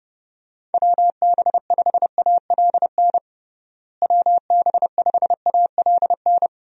Morse code, heard as a keyed tone, about 30 wpm, 700 Hz. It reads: WB5ALN WB5ALN
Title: USA Call Signs - 6 Characters - Repeated 2x - 30wpm